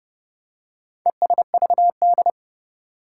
Morse code, heard as a keyed tone, about 30 wpm, 700 Hz. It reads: ESVD
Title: Four Challenging Letter-Number Combinations 30wpm